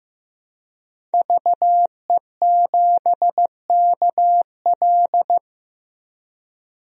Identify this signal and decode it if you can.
Morse code, heard as a keyed tone, about 15 wpm, 700 Hz. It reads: VE7KL
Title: DXCC-100 Call Signs - 5 Characters - Repeated 3x - 15wpm